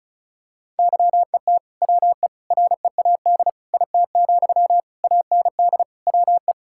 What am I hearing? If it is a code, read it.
Morse code, heard as a keyed tone, about 34 wpm, 700 Hz. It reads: YET WE READ IT, AND WE
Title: Sentences from Top 1600 Words 35wpm